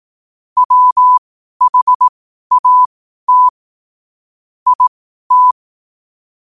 Morse code, heard as a keyed tone, about 18 wpm, spaced as for 12 wpm, 1000 Hz. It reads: WHAT IT
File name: qotdfiles / 12wpm